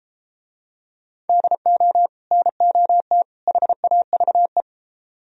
Morse code, heard as a keyed tone, about 33 wpm, 700 Hz. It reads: DO NOT HAVE